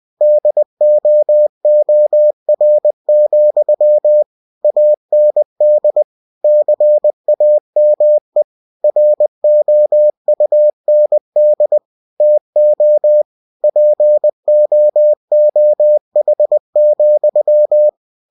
Morse code, heard as a keyed tone, about 20 wpm, 600 Hz. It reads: DOOR, AND CAME ROUND TO POOH,
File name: cw-books / winnie-the-pooh / winnie-20wpm-page-0016